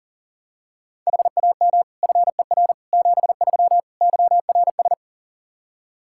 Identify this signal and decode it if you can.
Morse code, heard as a keyed tone, about 40 wpm, 700 Hz. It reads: HAM FER 73 YRS
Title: Practice QSO - Exchange 3 - More Info - 40wpm